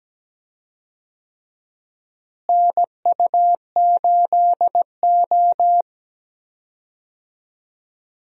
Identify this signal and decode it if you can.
Morse code, heard as a keyed tone, about 17 wpm, 700 Hz. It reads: NU8O
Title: USA Call Signs - 4 Characters - Repeated 3x - 17wpm